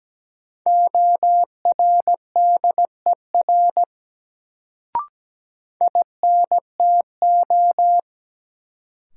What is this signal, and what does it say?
Morse code, heard as a keyed tone, about 17 wpm, 700 Hz. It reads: ORDER INTO
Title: Sets of 2 Words - New Word Drill - 17wpm